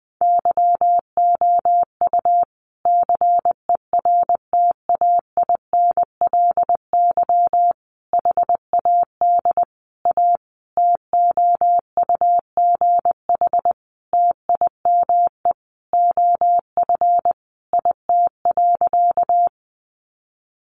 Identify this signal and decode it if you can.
Morse code, heard as a keyed tone, about 20 wpm, 700 Hz. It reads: YOU CERTAINLY HAD A TOUGH TIME OF IT.
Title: Sentences from Top 1200 Words 20wpm